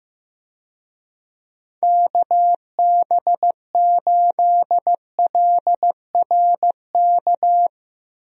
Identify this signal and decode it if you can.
Morse code, heard as a keyed tone, about 15 wpm, 700 Hz. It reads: KB8LRK
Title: USA Call Signs - 6 Characters - Repeated 3x - 15wpm